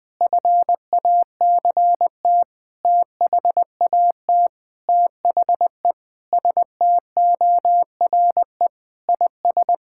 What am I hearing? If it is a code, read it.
Morse code, heard as a keyed tone, about 20 wpm, 700 Hz. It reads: FACT THAT THE STORE IS
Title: Sentences with 7-Word Phrases - 20wpm